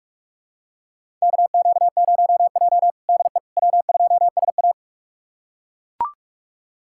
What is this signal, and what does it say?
Morse code, heard as a keyed tone, about 42 wpm, 700 Hz. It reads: KQ0J DE W2SA
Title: USA Callsign DE Callsign - 4 Characters - 45wpm